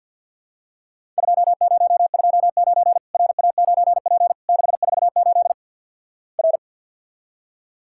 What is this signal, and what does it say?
Morse code, heard as a keyed tone, about 50 wpm, 700 Hz. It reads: 2029 RA9P 648 R
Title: Worked All Europe DX Contest - QTC Exchange - 50wpm